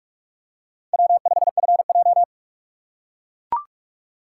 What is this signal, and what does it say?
Morse code, heard as a keyed tone, about 39 wpm, 700 Hz. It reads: W5FJ